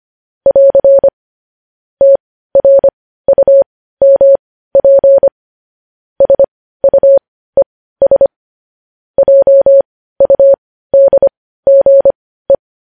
Morse code, heard as a keyed tone, about 25 wpm, spaced as for 15 wpm, 550 Hz. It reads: <AR> TRUMP SUES JUDGE
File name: podcast / News headlines-1x-25x15